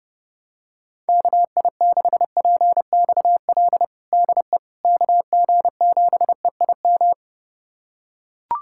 Morse code, heard as a keyed tone, about 32 wpm, 700 Hz. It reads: KI6PXL DE KG7EIM
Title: USA Callsign DE Callsign 30wpm